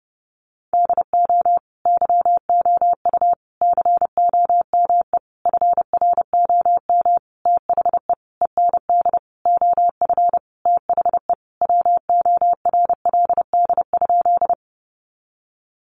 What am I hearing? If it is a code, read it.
Morse code, heard as a keyed tone, about 30 wpm, 700 Hz. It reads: DO YOU COME FROM THE END OF THE WORLD?
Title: Sentences from Top 200 Words 30wpm